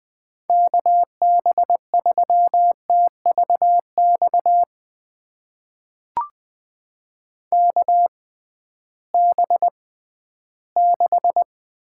Morse code, heard as a keyed tone, about 20 wpm, spaced as for 13 wpm, 700 Hz. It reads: KB3TVX KB6
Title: USA Call Signs - 6 Characters - 6x Character Spacing - 20wpm